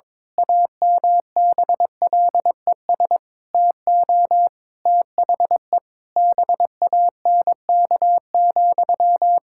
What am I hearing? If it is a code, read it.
Morse code, heard as a keyed tone, about 21 wpm, 700 Hz. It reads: AMBLES TO THE BANK,
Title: Pangram Challenge v2 - 22wpm